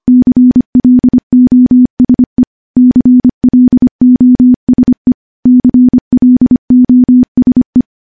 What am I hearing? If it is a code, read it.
Morse code, heard as a keyed tone, about 25 wpm, 250 Hz. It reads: CLOSE CLOSE CLOSE
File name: morse-training-25WPM